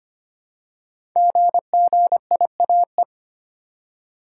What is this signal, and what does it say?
Morse code, heard as a keyed tone, about 25 wpm, 700 Hz. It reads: GGIAE
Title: Code Groups - 5 Letters - 25wpm